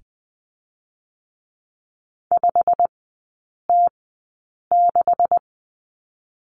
Morse code, heard as a keyed tone, about 20 wpm, 700 Hz. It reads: 5 T 6
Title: 212 Lesson 34e - Number 8 - Three Characters 20wpm